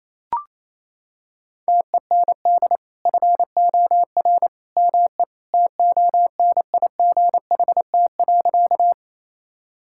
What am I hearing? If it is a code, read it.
Morse code, heard as a keyed tone, about 28 wpm, 700 Hz. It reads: TEND FOR ME TONIGHT.
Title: Sentences from Top 1000 Words - Review Part 5 - 28wpm